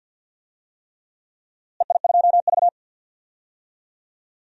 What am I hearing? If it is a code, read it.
Morse code, heard as a keyed tone, about 51 wpm, 700 Hz. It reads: EI2V
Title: DXCC-100 Call Signs - 4 Characters - Repeated 3x - 50wpm